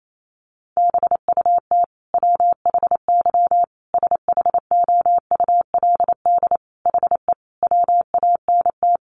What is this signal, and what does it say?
Morse code, heard as a keyed tone, about 28 wpm, 700 Hz. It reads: BUT WHY SHOULD HE WANT
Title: Sentences from Top 1200 Words 28wpm